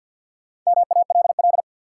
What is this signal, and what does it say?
Morse code, heard as a keyed tone, about 52 wpm, 700 Hz. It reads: MALL
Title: Four Letter Words - 50wpm